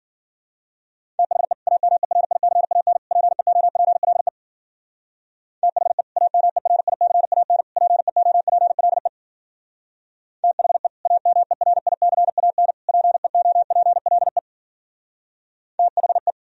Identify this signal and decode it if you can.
Morse code, heard as a keyed tone, about 51 wpm, 700 Hz. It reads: THE AMERICAN PEOPLE THE AMERICAN PEOPLE THE AMERICAN PEOPLE THE